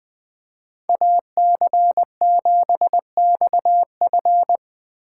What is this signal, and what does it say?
Morse code, heard as a keyed tone, about 19 wpm, 700 Hz. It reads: AC7XF
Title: Speed Racing - USA Call Signs 17wpm